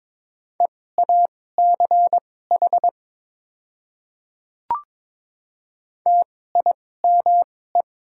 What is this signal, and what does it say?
Morse code, heard as a keyed tone, about 22 wpm, spaced as for 16 wpm, 700 Hz. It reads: EACH TIME
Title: 2x Character Spacing - Top 100 Words - 22wpm